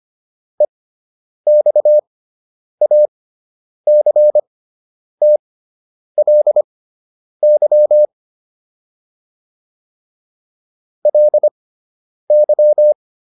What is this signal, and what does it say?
Morse code, heard as a keyed tone, about 25 wpm, spaced as for 9 wpm, 600 Hz. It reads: EXACTLY LY
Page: fundamental-practice-files